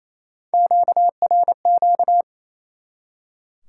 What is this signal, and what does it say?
Morse code, heard as a keyed tone, about 28 wpm, 700 Hz. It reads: QRQ
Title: Top 90 QSO Elements 28wpm